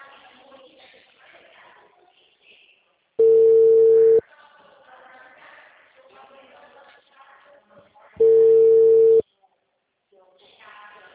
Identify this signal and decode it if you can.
Ringback tone (China, 450 Hz), with speech.
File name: out- -2020